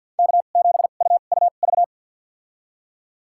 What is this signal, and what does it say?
Morse code, heard as a keyed tone, about 51 wpm, spaced as for 36 wpm, 700 Hz. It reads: K7UUV
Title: USA Call Signs - 2x Character Spacing - 50wpm